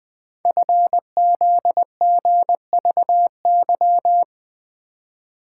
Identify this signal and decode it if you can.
Morse code, heard as a keyed tone, about 20 wpm, 700 Hz. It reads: FZGVY
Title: Code Groups - 5 Letters - 20wpm